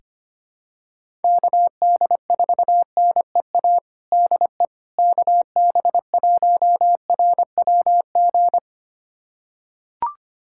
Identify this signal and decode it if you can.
Morse code, heard as a keyed tone, about 25 wpm, 700 Hz. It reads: KD4NEA DE KB1RWG
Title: USA Callsign DE Callsign - 6 Characters - 25wpm